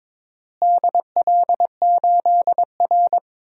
Morse code, heard as a keyed tone, about 22 wpm, 700 Hz. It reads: DL8R
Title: DXCC-100 Call Signs - 4 Characters - Repeated 2x - 22wpm